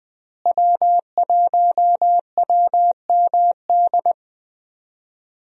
Morse code, heard as a keyed tone, about 20 wpm, 700 Hz. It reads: W1WMD
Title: USA Call Signs - 5 Characters - Repeated 3x - 20wpm